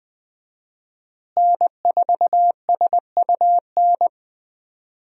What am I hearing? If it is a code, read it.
Morse code, heard as a keyed tone, about 20 wpm, 700 Hz. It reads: N4SUN
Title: USA Call Signs - 5 Characters - Repeated 3x - 20wpm